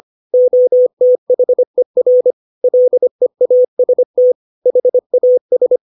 Morse code, heard as a keyed tone, about 25 wpm, 500 Hz. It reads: OTHER LEAST HAS
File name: 632 words / 632 Words 25WPM